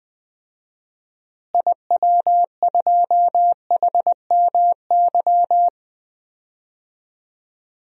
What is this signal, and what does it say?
Morse code, heard as a keyed tone, about 20 wpm, 700 Hz. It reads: IW2HMY
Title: DXCC-100 Call Signs - 6 Characters - Repeated 2x - 20wpm